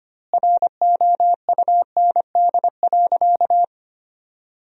Morse code, heard as a keyed tone, about 25 wpm, 700 Hz. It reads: ROUND.
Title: 2x Word-Spacing - Sentences from Top 200 Words - 25wpm